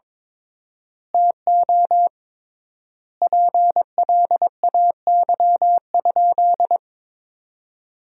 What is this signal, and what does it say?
Morse code, heard as a keyed tone, about 22 wpm, 700 Hz. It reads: TO PLAY?
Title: 3x Word-Spacing - Sentences from Top 400 Words - 22wpm